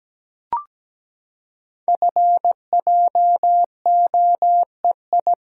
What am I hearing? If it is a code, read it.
Morse code, heard as a keyed tone, about 17 wpm, 700 Hz. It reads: FJOEI
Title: Code Groups - 5 Letters - 17wpm